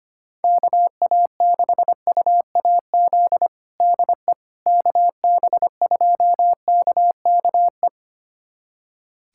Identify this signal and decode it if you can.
Morse code, heard as a keyed tone, about 25 wpm, 700 Hz. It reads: KA6UAZ DE KB2KKE
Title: USA Callsign DE Callsign - 6 Characters - 25wpm